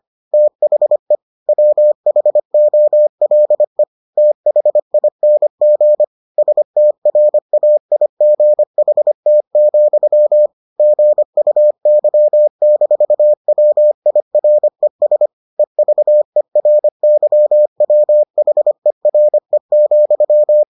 Morse code, heard as a keyed tone, about 25 wpm, 600 Hz. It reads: THE WHOLE THING STRAIGHT, GUY-WIRES EVERYWHERE,